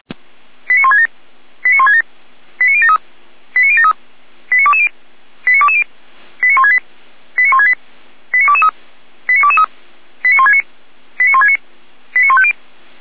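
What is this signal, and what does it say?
Five-tone selective call (ZVEI), heard as quick runs of five tones, tones of about 70 ms. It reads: 89167 89167 89073 89073 89220 89220 89168 89168 89223 89223 89169 89169 89150
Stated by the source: Received on SDR; ZVEI five-tone selective call.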